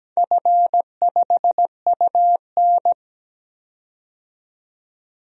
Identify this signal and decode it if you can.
Morse code, heard as a keyed tone, about 17 wpm, 700 Hz. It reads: F5UN
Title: DXCC-100 Call Signs - 4 Characters - 17wpm